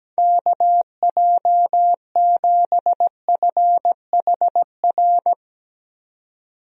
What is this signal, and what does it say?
Morse code, heard as a keyed tone, about 17 wpm, 700 Hz. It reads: KJ7FHR
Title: USA Call Signs - 6 Characters - 17wpm